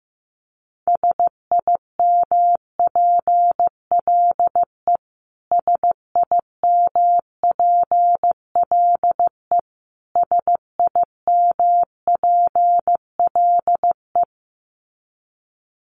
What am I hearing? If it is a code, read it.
Morse code, heard as a keyed tone, about 15 wpm, 700 Hz. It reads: SIMPLE SIMPLE SIMPLE
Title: Top 500 Words Repeated 15wpm